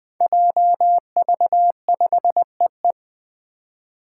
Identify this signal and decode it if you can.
Morse code, heard as a keyed tone, about 20 wpm, 700 Hz. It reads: JV5EE
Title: Five Challenging Letter-Number Combinations 20wpm